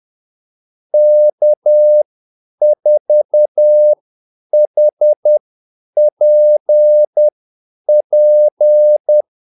Morse code, heard as a keyed tone, about 10 wpm, spaced as for 8 wpm, 600 Hz. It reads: K4HPP
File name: Callsigns-10-8wpm